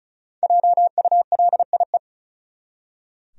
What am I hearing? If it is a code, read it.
Morse code, heard as a keyed tone, about 36 wpm, 700 Hz. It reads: JULIE
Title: USA Common Names - 35wpm